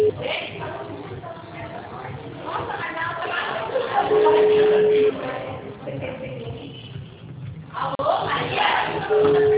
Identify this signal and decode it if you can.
Ringback tone (Europe, 425 Hz), with speech.
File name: out- -2012- - - .44423